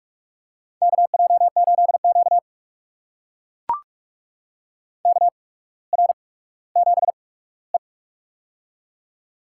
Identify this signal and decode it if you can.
Morse code, heard as a keyed tone, about 44 wpm, spaced as for 19 wpm, 700 Hz. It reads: KJ8Q KR7E
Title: USA Call Signs - 4 Characters - 8x Character Spacing - 45wpm